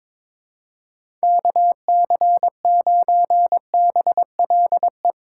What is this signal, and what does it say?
Morse code, heard as a keyed tone, about 22 wpm, 700 Hz. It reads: KC9BLE
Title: USA Call Signs - 6 Characters - 2x Character Spacing - 22wpm